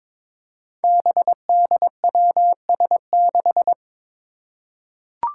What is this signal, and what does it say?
Morse code, heard as a keyed tone, about 22 wpm, 700 Hz. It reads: BDWS6
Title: Five Challenging Letter-Number Combinations 22wpm